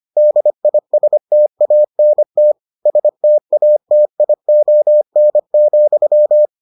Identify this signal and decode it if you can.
Morse code, heard as a keyed tone, about 25 wpm, 600 Hz. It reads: DISTANT STATION,